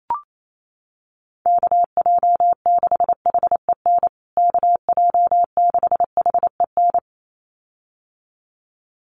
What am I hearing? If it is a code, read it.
Morse code, heard as a keyed tone, about 28 wpm, 700 Hz. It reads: KJ6HEN KJ6HEN
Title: USA Call Signs Repeated 2x 28wpm